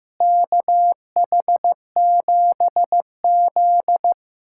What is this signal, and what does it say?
Morse code, heard as a keyed tone, about 15 wpm, 700 Hz. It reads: KH7Z
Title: USA Callsign DE Callsign - 4 Characters - 15wpm